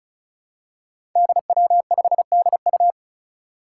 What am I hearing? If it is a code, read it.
Morse code, heard as a keyed tone, about 36 wpm, 700 Hz. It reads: DW5DU
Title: Five Challenging Letter-Number Combinations 35wpm